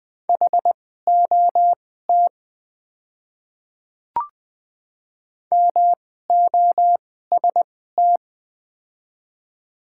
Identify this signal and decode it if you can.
Morse code, heard as a keyed tone, about 20 wpm, spaced as for 15 wpm, 700 Hz. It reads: HOT MOST
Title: 2x Character Spacing - Top 100 Words - 20wpm